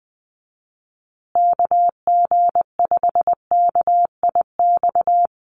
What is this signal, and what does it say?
Morse code, heard as a keyed tone, about 20 wpm, 700 Hz. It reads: KG5KIX
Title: USA Call Signs Repeated 2x 20wpm